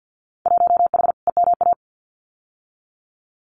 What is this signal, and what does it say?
Morse code, heard as a keyed tone, about 50 wpm, 700 Hz. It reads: 1H EMA
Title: Winter Field Day Exchanges 50wpm